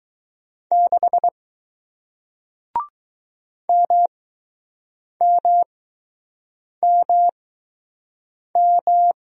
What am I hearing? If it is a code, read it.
Morse code, heard as a keyed tone, about 20 wpm, spaced as for 6 wpm, 700 Hz. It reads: 6 MMMM